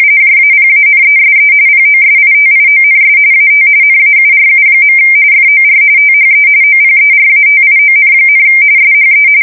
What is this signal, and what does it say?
RTTY, about 45.45 baud. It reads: JAPANESE POSITIONS AT NAURU. FOR THE OPERATION, SHE
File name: fsk170 / USS-Massachusetts